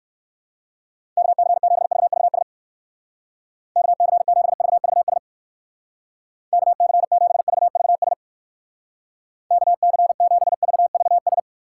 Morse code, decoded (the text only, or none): KC7VVS KC7VVS KC7VVS KC7VVS